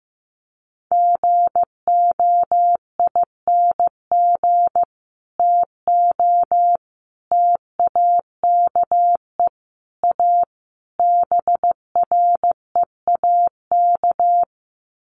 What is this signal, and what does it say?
Morse code, heard as a keyed tone, about 15 wpm, 700 Hz. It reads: GOING TO TAKE A BREAK
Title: Sets of 5 Words 15wpm